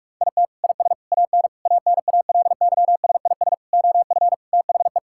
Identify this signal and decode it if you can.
Morse code, heard as a keyed tone, about 46 wpm, 700 Hz. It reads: IT IS AN ANALYSIS OF THE
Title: Sentences from Top 1000 Words - Review Part 5 - 45wpm